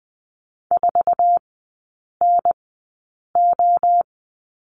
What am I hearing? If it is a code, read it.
Morse code, heard as a keyed tone, about 20 wpm, spaced as for 8 wpm, 700 Hz. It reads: 4NO
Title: 060 Lesson 11e - Letter h - Three Characters 20wpm